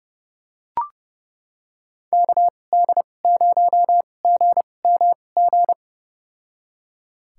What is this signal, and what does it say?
Morse code, heard as a keyed tone, about 30 wpm, spaced as for 22 wpm, 700 Hz. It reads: KD0GMG
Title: USA Call Signs - 6 Characters - 2x Character Spacing - 30wpm